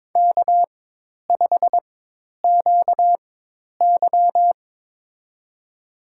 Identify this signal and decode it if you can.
Morse code, heard as a keyed tone, about 22 wpm, spaced as for 10 wpm, 700 Hz. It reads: K5QY
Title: USA Call Signs - 4 Characters - 4x Character Spacing - 22wpm